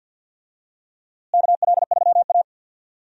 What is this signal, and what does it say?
Morse code, heard as a keyed tone, about 51 wpm, 700 Hz. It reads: KL3A
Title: USA Call Signs - 4 Characters - Repeated 2x - 50wpm